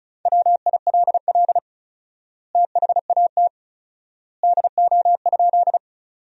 Morse code, heard as a keyed tone, about 36 wpm, 700 Hz. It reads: WILL THAT DO?
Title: 4x Word-Spacing - Sentences from Top 100 Words - 35wpm